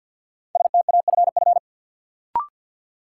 Morse code, heard as a keyed tone, about 52 wpm, 700 Hz. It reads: STAFF
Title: Sentences from Top 1200 Words - New Word Drill Alt - 50wpm